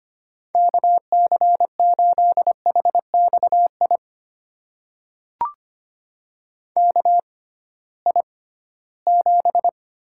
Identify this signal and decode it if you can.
Morse code, heard as a keyed tone, about 25 wpm, spaced as for 16 wpm, 700 Hz. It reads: KC8HXI KI7
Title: USA Call Signs - 6 Characters - 6x Character Spacing - 25wpm